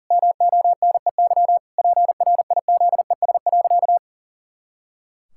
Morse code, heard as a keyed tone, about 37 wpm, 700 Hz. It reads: MONEY PRIZES.
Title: Pangram Challenge v2 - 40wpm